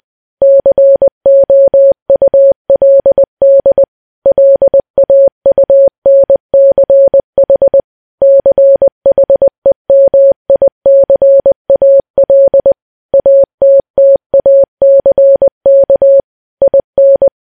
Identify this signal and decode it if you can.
Morse code, heard as a keyed tone, about 20 wpm, 550 Hz. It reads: COULD LAUNCH CHEMICAL ATTACK IN
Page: news-headlines-in-morse-code-at-20-wpm